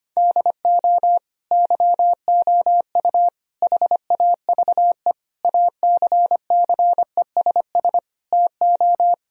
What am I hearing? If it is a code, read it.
Morse code, heard as a keyed tone, about 25 wpm, 700 Hz. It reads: DO YOU HAVE ACCESS TO
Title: Sentences from Top 1000 Words - Review Part 4 - 25wpm